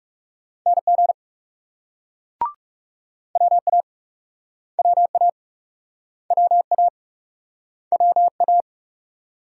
Morse code, heard as a keyed tone, about 42 wpm, 700 Hz. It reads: NG WA WA WA WA